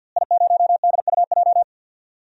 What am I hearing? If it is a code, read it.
Morse code, heard as a keyed tone, about 48 wpm, 700 Hz. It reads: I0DUJ
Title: DXCC-100 Call Signs - 5 Characters - Repeated 2x - 50wpm